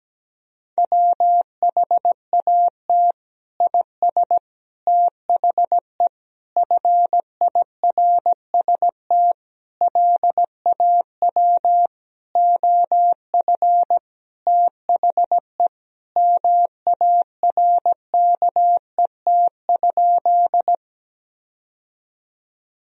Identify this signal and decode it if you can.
Morse code, heard as a keyed tone, about 17 wpm, 700 Hz. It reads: WHAT IS THE FIRST LAW OF THE MARKET?
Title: Sentences from Top 1000 Words - Review Part 3 - 17wpm